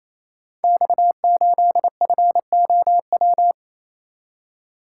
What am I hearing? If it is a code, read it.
Morse code, heard as a keyed tone, about 28 wpm, 700 Hz. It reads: X8FOW
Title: Code Groups - 5 Letter:Numbers - 28wpm